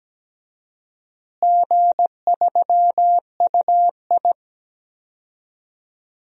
Morse code, heard as a keyed tone, about 17 wpm, 700 Hz. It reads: G3UI
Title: DXCC-100 Call Signs - 4 Characters - Repeated 2x - 17wpm